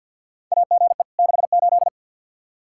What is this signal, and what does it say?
Morse code, heard as a keyed tone, about 52 wpm, 700 Hz. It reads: AGE 68
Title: Practice QSO - Exchange 3 - More Info - 50wpm